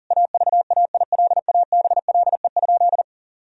Morse code, heard as a keyed tone, about 40 wpm, 700 Hz. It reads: AVAILABLE?
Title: Sentences from Top 1000 Words - Review Part 3 - 40wpm